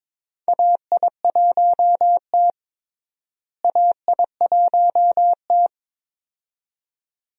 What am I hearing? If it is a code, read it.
Morse code, heard as a keyed tone, about 22 wpm, 700 Hz. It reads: AI1T AI1T
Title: USA Call Signs - 4 Characters - Repeated 2x - 22wpm